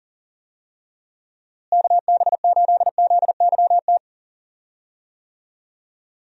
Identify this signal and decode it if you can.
Morse code, heard as a keyed tone, about 40 wpm, 700 Hz. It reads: KB8ZYT
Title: USA Call Signs - 6 Characters - Repeated 3x - 40wpm